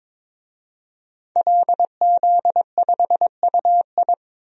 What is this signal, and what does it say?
Morse code, heard as a keyed tone, about 22 wpm, 700 Hz. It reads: LZ5UI
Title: DXCC-100 Call Signs - 5 Characters - 22wpm